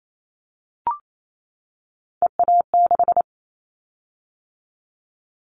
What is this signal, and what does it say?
Morse code, heard as a keyed tone, about 28 wpm, 700 Hz. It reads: EA6
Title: Callsign Prefixes for Top 100 DXCC - 28wpm